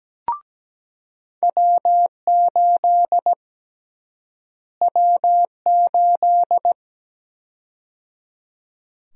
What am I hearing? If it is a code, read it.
Morse code, heard as a keyed tone, about 17 wpm, 700 Hz. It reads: W8 W8
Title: USA Call Sign Prefixes - Repeated 2x - 17wpm